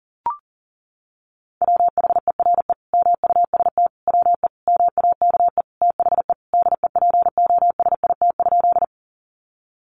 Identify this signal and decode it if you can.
Morse code, heard as a keyed tone, about 38 wpm, 700 Hz. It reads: WHERE MUST WE MAKE THE DEPOSIT?
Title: Sentences from Top 1300 Words 40wpm